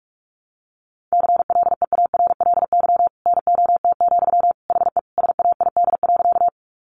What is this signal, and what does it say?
Morse code, heard as a keyed tone, about 44 wpm, 700 Hz. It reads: CLEARLY NOT, HE SAID.
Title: Sentences from Top 1300 Words 45wpm